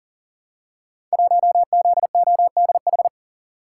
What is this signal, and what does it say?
Morse code, heard as a keyed tone, about 39 wpm, 700 Hz. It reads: 1ZODH
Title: Code Groups - 5 Letter:Numbers - 40wpm